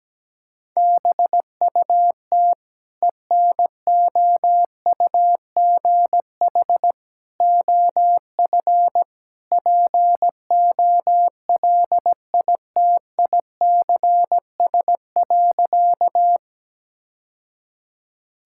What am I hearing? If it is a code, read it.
Morse code, heard as a keyed tone, about 17 wpm, 700 Hz. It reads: BUT ENOUGH OF POLITICS.
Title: Sentences from Top 1000 Words - Review Part 6 - 17wpm